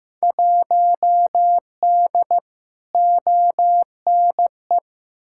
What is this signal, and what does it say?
Morse code, heard as a keyed tone, about 15 wpm, 700 Hz. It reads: 1D ONE
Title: Field Day Exchanges - 15wpm